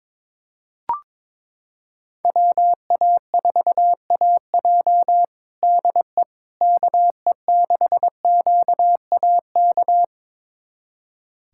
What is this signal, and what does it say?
Morse code, heard as a keyed tone, about 22 wpm, 700 Hz. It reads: WA4AJ DE KE6QAK
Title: USA Callsign DE Callsign 22wpm